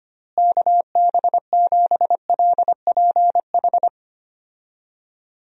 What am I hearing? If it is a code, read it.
Morse code, heard as a keyed tone, about 25 wpm, 700 Hz. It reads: KB7LPH
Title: USA Call Signs - 6 Characters - Repeated 3x - 25wpm